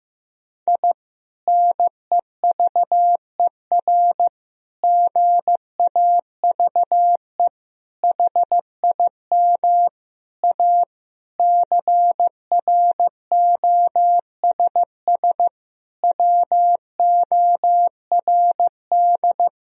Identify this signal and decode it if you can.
Morse code, heard as a keyed tone, about 15 wpm, 700 Hz. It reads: I NEVER GAVE HIM A CROSS WORD
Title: Sentences from Top 1000 Words - Review Part 1 - 15wpm